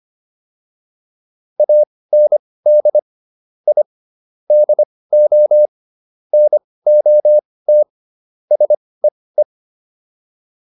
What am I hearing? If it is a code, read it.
Morse code, heard as a keyed tone, about 25 wpm, spaced as for 18 wpm, 600 Hz. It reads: AND I DO NOT SEE